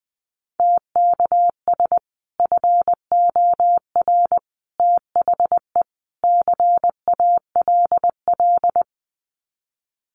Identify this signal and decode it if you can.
Morse code, heard as a keyed tone, about 20 wpm, 700 Hz. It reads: TKS FOR THE CALL
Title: Common QSO Phrases 20wpm